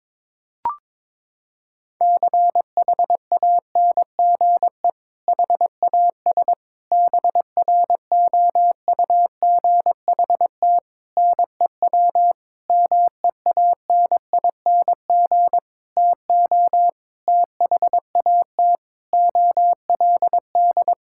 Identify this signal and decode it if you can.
Morse code, heard as a keyed tone, about 22 wpm, 700 Hz. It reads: CHANGE HAS BROUGHT NEW MEANING TO THAT OLD
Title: Sentences from Top 1600 Words 22wpm